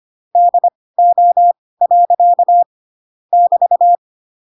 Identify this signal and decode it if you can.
Morse code, heard as a keyed tone, about 25 wpm, spaced as for 18 wpm, 700 Hz. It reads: DO. <BT>
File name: Pinocchio / Pinocchio ch4 12 25 ik1hge